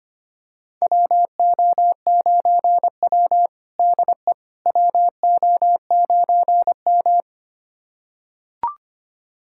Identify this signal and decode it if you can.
Morse code, heard as a keyed tone, about 25 wpm, 700 Hz. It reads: WO9W DE WO9M
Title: USA Callsign DE Callsign - 4 Characters - 25wpm